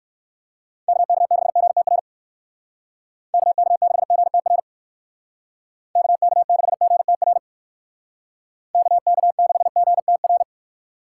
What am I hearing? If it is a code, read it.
Morse code, heard as a keyed tone, about 58 wpm, 700 Hz. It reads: KK6GTR KK6GTR KK6GTR KK6GTR